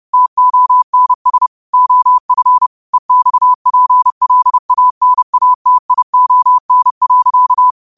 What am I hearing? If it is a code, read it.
Morse code, heard as a keyed tone, about 30 wpm, 1000 Hz. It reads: TONS OF EXPLANATION.
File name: qotdfiles / 30wpm